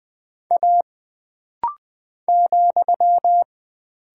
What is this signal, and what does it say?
Morse code, heard as a keyed tone, about 20 wpm, 700 Hz. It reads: A ,
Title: Single Character (Letter:Number:Punctuation:Prosign) - Rapid-Fire - 20wpm